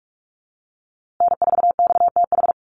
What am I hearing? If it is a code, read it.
Morse code, heard as a keyed tone, about 44 wpm, 700 Hz. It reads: N4XTH
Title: USA Call Signs Repeated 3x 45wpm